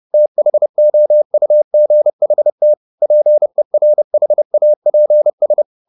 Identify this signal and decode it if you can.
Morse code, heard as a keyed tone, about 30 wpm, 600 Hz. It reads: THOUGHT PERHAPS